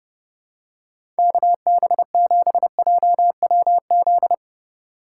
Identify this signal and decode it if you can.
Morse code, heard as a keyed tone, about 30 wpm, 700 Hz. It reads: KB7JWZ
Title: USA Call Signs - 6 Characters - 3x Character Spacing - 30wpm